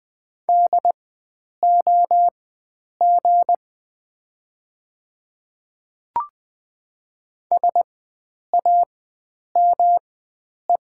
Morse code, heard as a keyed tone, about 20 wpm, spaced as for 9 wpm, 700 Hz. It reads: DOG SAME
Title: 4x Character Spacing - Top 500 Words - 20wpm